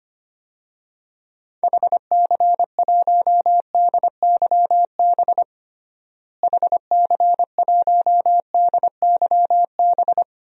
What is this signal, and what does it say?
Morse code, heard as a keyed tone, about 25 wpm, 700 Hz. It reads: HC1DYB HC1DYB
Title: DXCC-100 Call Signs - 6 Characters - Repeated 2x - 25wpm